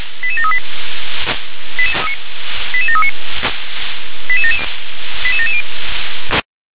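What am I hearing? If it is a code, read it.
Five-tone selective call (ZVEI), heard as quick runs of five tones, tones of about 70 ms. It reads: 88738 88739 88770
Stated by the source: ZVEI five-tone call received by SDR.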